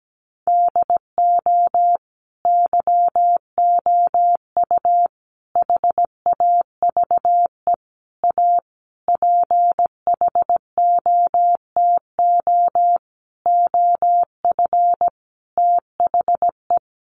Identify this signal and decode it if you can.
Morse code, heard as a keyed tone, about 17 wpm, 700 Hz. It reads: DO YOU HAVE A PHOTO OF THE
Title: Sentences from Top 1200 Words 17wpm